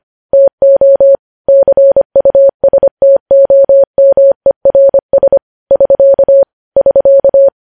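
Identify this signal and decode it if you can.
Morse code, heard as a keyed tone, about 25 wpm, 550 Hz. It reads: TO CUSTOMERS <SK> <SK>
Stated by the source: News headlines sent in Morse code at 25.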